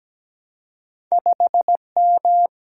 Morse code, heard as a keyed tone, about 17 wpm, 700 Hz. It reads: 5M
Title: Single-Letter Single-Number 17wpm